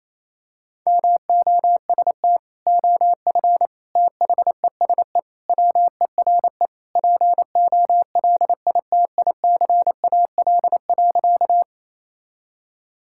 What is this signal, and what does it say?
Morse code, heard as a keyed tone, about 28 wpm, 700 Hz. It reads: MOST OF THESE WERE POLITICAL.
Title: Sentences from Top 1000 Words - Review Part 5 - 28wpm